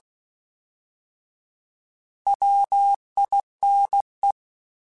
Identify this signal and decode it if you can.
Morse code, heard as a keyed tone, about 16 wpm, 800 Hz. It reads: WINE
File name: Alphabetical / Alphabetical mp3s / Ws @15